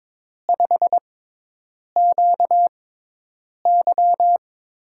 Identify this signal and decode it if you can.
Morse code, heard as a keyed tone, about 21 wpm, spaced as for 8 wpm, 700 Hz. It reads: 5QY
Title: USA Call Signs - 6x Character Spacing - 22wpm